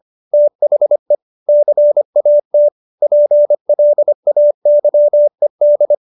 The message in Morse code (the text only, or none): THE CAT PLAYED